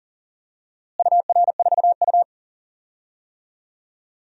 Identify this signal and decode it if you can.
Morse code, heard as a keyed tone, about 40 wpm, 700 Hz. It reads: UR4U
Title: DXCC-100 Call Signs - 4 Characters - Repeated 2x - 40wpm